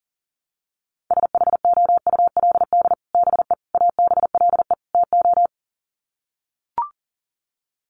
Morse code, heard as a keyed tone, about 40 wpm, 700 Hz. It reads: SHOULD BE ABLE TO
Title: Sets of 4 Words 40wpm